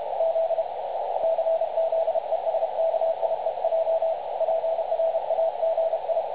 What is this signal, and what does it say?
Morse code, heard as a keyed tone, about 30 wpm, 650 Hz, with radio noise. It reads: Z CZESKIEJ PRAGI
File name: dlazaawansowanych / nagraniazpasma / websdr recording 3565.0kHz